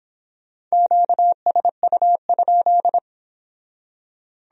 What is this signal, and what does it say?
Morse code, heard as a keyed tone, about 30 wpm, 700 Hz. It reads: QSU?